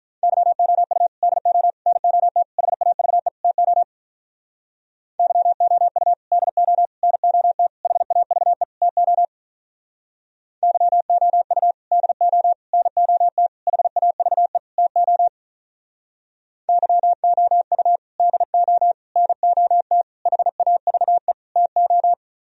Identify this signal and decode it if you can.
Morse code, heard as a keyed tone, about 44 wpm, 700 Hz. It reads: YOU DO NOT HAVE TO YOU DO NOT HAVE TO YOU DO NOT HAVE TO YOU DO NOT HAVE TO